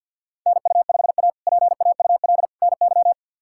Morse code, heard as a keyed tone, about 47 wpm, 700 Hz. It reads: NU5A PAUL NY
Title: K1USN SST QSO Practice - 50wpm